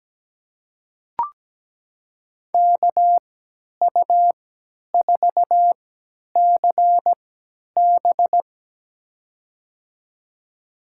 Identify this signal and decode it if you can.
Morse code, heard as a keyed tone, about 17 wpm, spaced as for 10 wpm, 700 Hz. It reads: KU4CB
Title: USA Call Signs - 3x Character Spacing - 17wpm